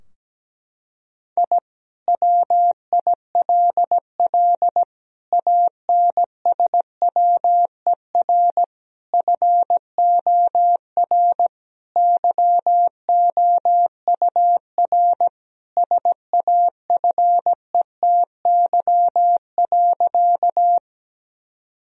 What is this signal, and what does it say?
Morse code, heard as a keyed tone, about 17 wpm, 700 Hz. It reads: I WILL ANSWER FOR YOUR SAFETY.
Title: Sentences from Top 1000 Words - Review Part 4 - 17wpm